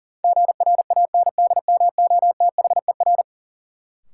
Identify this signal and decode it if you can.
Morse code, heard as a keyed tone, about 40 wpm, 700 Hz. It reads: GRANDMOTHER
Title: Sentences from Top 1200 Words - New Word Drill Alt - 40wpm